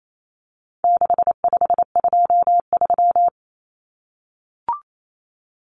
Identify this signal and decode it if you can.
Morse code, heard as a keyed tone, about 28 wpm, 700 Hz. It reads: 6523
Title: Numbers - 4-Digit 28wpm